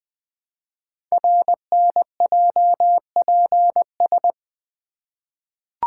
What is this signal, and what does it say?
Morse code, heard as a keyed tone, about 20 wpm, 700 Hz. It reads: RNJPS
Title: Code Groups - 5 Letters - 20wpm